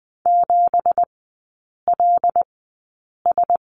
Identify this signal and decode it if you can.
Morse code, heard as a keyed tone, about 20 wpm, spaced as for 8 wpm, 700 Hz. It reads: 7LS
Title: 205 Lesson 33e - Letter j - Three Characters 20wpm